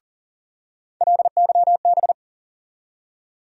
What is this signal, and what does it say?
Morse code, heard as a keyed tone, about 40 wpm, 700 Hz. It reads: LYB
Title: USA Call Sign Postfixes - 3 Letters 40wpm